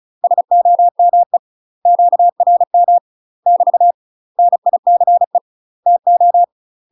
Morse code, heard as a keyed tone, about 33 wpm, 700 Hz. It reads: SOME QRM <BT> NICE TO
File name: UK MASTER QSOs TRAINING / 35WPM / UK MASTER QSOs 0038